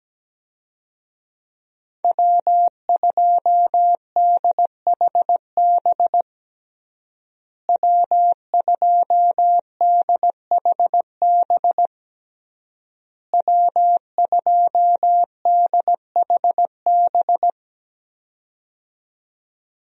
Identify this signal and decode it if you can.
Morse code, heard as a keyed tone, about 17 wpm, 700 Hz. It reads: W2DHB W2DHB W2DHB
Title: USA Call Signs - 5 Characters - Repeated 3x - 17wpm